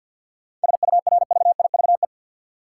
Morse code, heard as a keyed tone, about 52 wpm, 700 Hz. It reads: SURVIVE
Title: Seven Letter Words - 50wpm